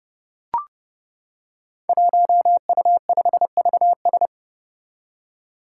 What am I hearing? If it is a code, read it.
Morse code, heard as a keyed tone, about 30 wpm, 700 Hz. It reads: 1U5VS
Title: Five Challenging Letter-Number Combinations 30wpm